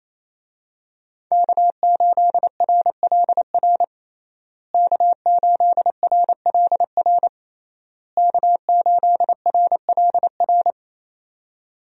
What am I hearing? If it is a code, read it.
Morse code, heard as a keyed tone, about 28 wpm, 700 Hz. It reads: K8RLR K8RLR K8RLR
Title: USA Call Signs - 5 Characters - Repeated 3x - 28wpm